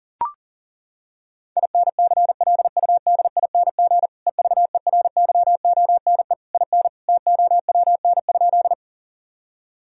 Morse code, heard as a keyed tone, about 40 wpm, 700 Hz. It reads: INCLUDING EVERYONE IN TOWN?
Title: Sentences from Top 1000 Words - Review Part 4 - 40wpm